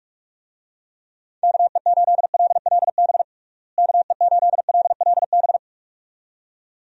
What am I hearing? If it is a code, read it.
Morse code, heard as a keyed tone, about 42 wpm, 700 Hz. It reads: KE8LLB KE8LLB
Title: USA Call Signs - 6 Characters - Repeated 2x - 45wpm